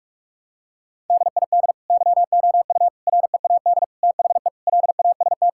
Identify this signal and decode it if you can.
Morse code, heard as a keyed tone, about 46 wpm, 700 Hz. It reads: DID YOU READ THE LAST
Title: Sentences from Top 1000 Words - Review Part 5 - 45wpm